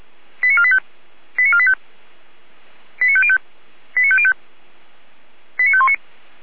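Five-tone selective call (ZVEI), heard as quick runs of five tones, tones of about 70 ms. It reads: 89485 89485 89505 89505 89519
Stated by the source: ZVEI five-tone selective call received on SDR.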